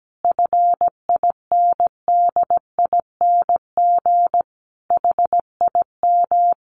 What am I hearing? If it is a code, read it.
Morse code, heard as a keyed tone, about 17 wpm, 700 Hz. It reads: FINDING HIM
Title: Sentences from Top 1100 Words 17wpm